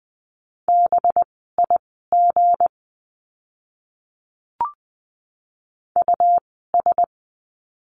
Morse code, heard as a keyed tone, about 20 wpm, spaced as for 15 wpm, 700 Hz. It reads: BIG US
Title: 192 Lesson 31f - Letter v - Words 20wpm